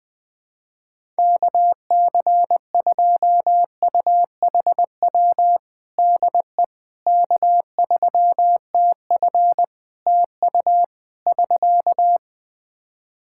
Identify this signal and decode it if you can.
Morse code, heard as a keyed tone, about 20 wpm, 700 Hz. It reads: KC2UHW DE K3TF TU <SK>
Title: Practice QSO - Exchange 4 - Ending - 20wpm